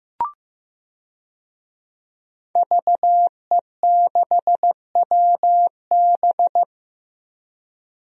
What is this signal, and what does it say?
Morse code, heard as a keyed tone, about 15 wpm, 700 Hz. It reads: VE6WB
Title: DXCC-100 Call Signs - 5 Characters - 15wpm